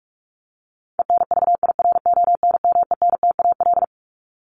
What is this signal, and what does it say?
Morse code, heard as a keyed tone, about 44 wpm, 700 Hz. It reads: ENVIRONMENTAL
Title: Top 1001-1250 Words 45wpm